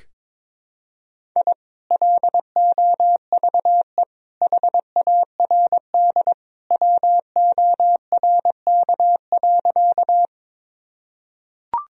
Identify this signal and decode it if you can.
Morse code, heard as a keyed tone, about 22 wpm, 700 Hz. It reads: I LOVE HARD WORK.
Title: Sentences from Top 1000 Words - Review Part 1 - 22wpm